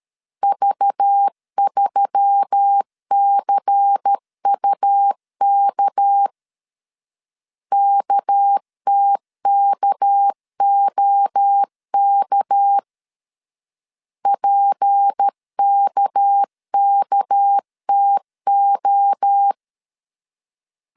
Morse code, heard as a keyed tone, about 13 wpm, 800 Hz. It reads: V3CUK KTKOK PKKTO
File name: LEZIONI / 6 PKX0 / 6 PK 12wpm 2006